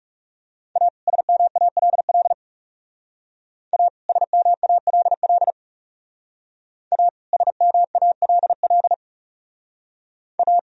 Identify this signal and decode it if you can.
Morse code, heard as a keyed tone, about 36 wpm, 700 Hz. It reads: A SMALL A SMALL A SMALL A
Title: Speed-Racing - Sets of 2 Words 30wpm